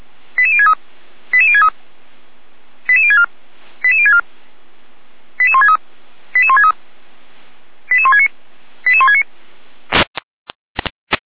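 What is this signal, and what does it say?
Five-tone selective call (ZVEI), heard as quick runs of five tones, tones of about 70 ms. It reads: 88063 88063 88064 88064 88163 88163 88169 88169
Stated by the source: Five-tone ZVEI selective call received by SDR, from feuerwehr-orlamuende.org (Orlamünde fire brigade) server.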